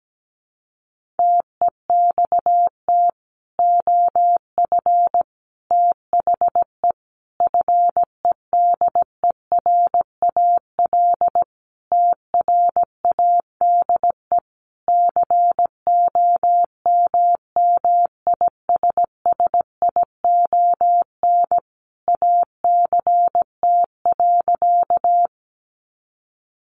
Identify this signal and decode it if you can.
Morse code, heard as a keyed tone, about 17 wpm, 700 Hz. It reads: TEXT OF THE FEDERAL TRADE COMMISSION ACT.
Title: Sentences from Top 1400 Words 17wpm